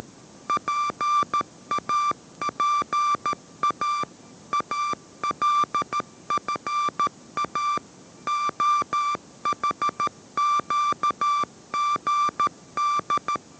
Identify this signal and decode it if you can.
Morse code, heard as a keyed tone, about 13 wpm, 1180 Hz. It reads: PAPA ALFA OHQGD
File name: 14wpm